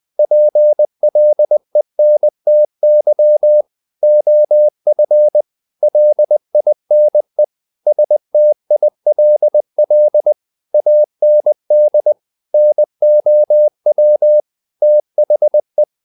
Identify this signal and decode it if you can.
Morse code, heard as a keyed tone, about 20 wpm, 600 Hz. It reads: PLENTY OF LINE STILL AND NOW THE
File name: the old man and the sea part 2 20wpm